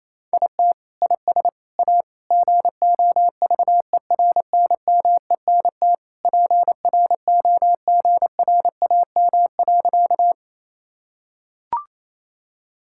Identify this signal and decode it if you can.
Morse code, heard as a keyed tone, about 28 wpm, 700 Hz. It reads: IT IS A GOVERNMENT PROGRAM.
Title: Sentences from Top 1000 Words - Review Part 5 - 28wpm